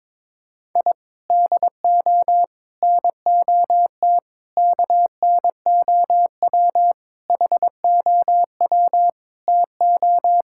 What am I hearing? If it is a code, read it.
Morse code, heard as a keyed tone, about 22 wpm, 700 Hz. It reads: I DO NOT KNOW HOW TO
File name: practice / 22wpm / Sentences with 4-Word Phrases - 22wpm